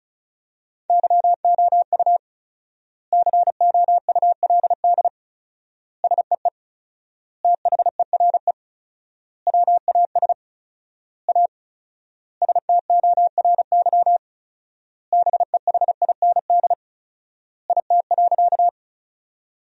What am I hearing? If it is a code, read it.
Morse code, heard as a keyed tone, about 36 wpm, 700 Hz. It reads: YOU COULD SEE THERE WAS A STORY BEHIND IT.
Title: 4x Word-Spacing - Sentences from Top 500 Words - 35wpm